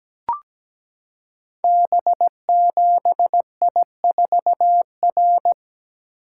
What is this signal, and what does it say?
Morse code, heard as a keyed tone, about 17 wpm, 700 Hz. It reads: B7I4R
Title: Code Groups - 5 Letter:Numbers - 17wpm